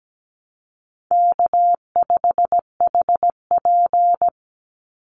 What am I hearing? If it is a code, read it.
Morse code, heard as a keyed tone, about 17 wpm, 700 Hz. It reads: K5HP
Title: Easy Call Signs - 17wpm